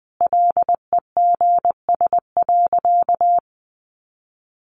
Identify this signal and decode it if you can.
Morse code, heard as a keyed tone, about 20 wpm, 700 Hz. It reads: LEGS.
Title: A Sound Of Thunder 20wpm